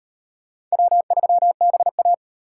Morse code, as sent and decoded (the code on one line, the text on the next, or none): .-- ...-- -... .-
W3BA